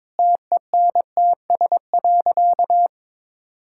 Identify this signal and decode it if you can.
Morse code, heard as a keyed tone, about 22 wpm, 700 Hz. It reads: TENTS.
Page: morse-code-ninja-practice-22wpm